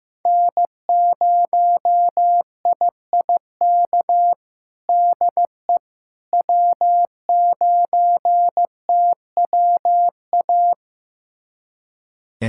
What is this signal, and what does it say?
Morse code, heard as a keyed tone, about 15 wpm, 700 Hz. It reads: N0IIK DE W9TWA
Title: USA Callsign DE Callsign - 5 Characters - 15wpm